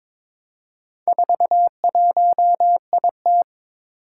Morse code, heard as a keyed tone, about 22 wpm, 700 Hz. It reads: 41IT
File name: practice / 22wpm / Four Letter-Number Combinations - 22wpm